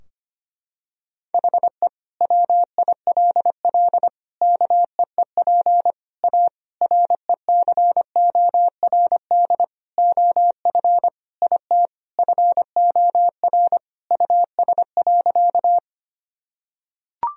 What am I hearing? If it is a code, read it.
Morse code, heard as a keyed tone, about 25 wpm, 700 Hz. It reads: HE WILL KEEP A RECORD OF IT FOR US.
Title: Sentences from Top 1000 Words - Review Part 3 - 25wpm